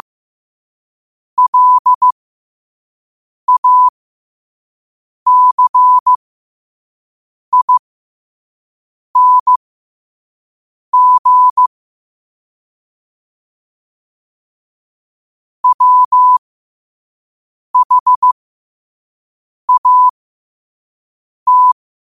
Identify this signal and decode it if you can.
Morse code, heard as a keyed tone, about 15 wpm, spaced as for 5 wpm, 1000 Hz. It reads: LACING WHAT
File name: qotdfiles / 5wpm